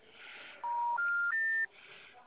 Special information tone.
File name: out- -2044- - - .1745